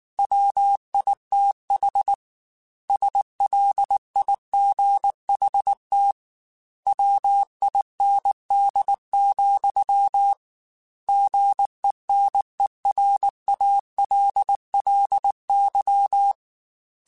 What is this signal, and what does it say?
Morse code, heard as a keyed tone, about 19 wpm, 800 Hz. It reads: WITH SLIGHT WIND, GENERALLY